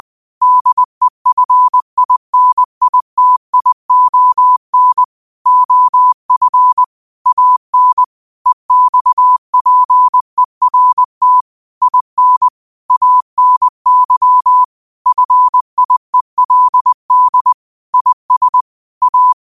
Morse code, heard as a keyed tone, about 20 wpm, 1000 Hz. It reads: DEFINITION OF AN EXPERT IN ANY FIELD IS A
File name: qotdfiles / 20wpm